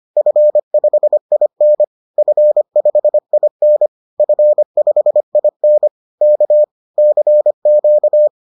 Morse code, heard as a keyed tone, about 25 wpm, 600 Hz. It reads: F5IN F5IN F5IN K CQ